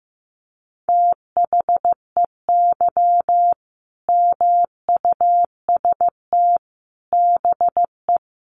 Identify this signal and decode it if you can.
Morse code, heard as a keyed tone, about 15 wpm, 700 Hz. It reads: THEY MUST BE
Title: Sentences from Top 1400 Words 15wpm